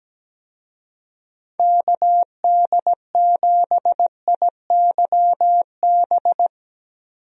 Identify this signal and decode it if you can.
Morse code, heard as a keyed tone, about 17 wpm, 700 Hz. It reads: KD7IYB
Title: USA Call Signs - 6 Characters - 17wpm